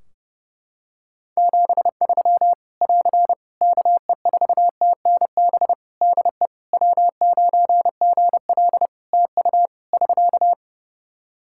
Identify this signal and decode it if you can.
Morse code, heard as a keyed tone, about 30 wpm, 700 Hz. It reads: 73 <AR> KE4TNB DE W9GL TU <SK>
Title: K1USN SST QSO Practice 30wpm